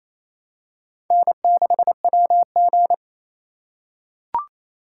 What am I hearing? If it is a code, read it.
Morse code, heard as a keyed tone, about 28 wpm, 700 Hz. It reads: N6WG
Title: USA Call Signs - 4 Characters - 3x Character Spacing - 28wpm